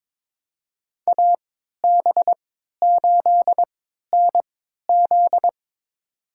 Morse code, heard as a keyed tone, about 22 wpm, spaced as for 13 wpm, 700 Hz. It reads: AB8NZ
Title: USA Call Signs - 5 Characters - 3x Character Spacing - 22wpm